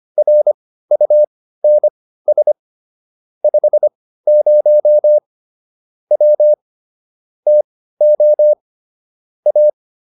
Morse code, heard as a keyed tone, about 25 wpm, spaced as for 15 wpm, 600 Hz. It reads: RUNS 50 W TO A
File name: QSO 302 15